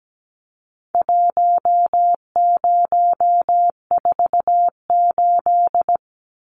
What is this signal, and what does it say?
Morse code, heard as a keyed tone, about 17 wpm, 700 Hz. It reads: 1048
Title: Numbers - Contest Style - 4-Digit 17wpm